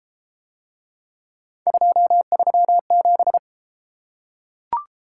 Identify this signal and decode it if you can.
Morse code, heard as a keyed tone, about 32 wpm, 700 Hz. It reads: 237